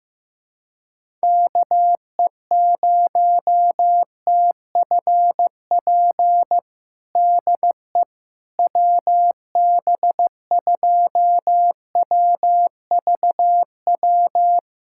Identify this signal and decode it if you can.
Morse code, heard as a keyed tone, about 15 wpm, 700 Hz. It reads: KE0TFP DE WB2WVW
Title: USA Callsign DE Callsign - 6 Characters - 15wpm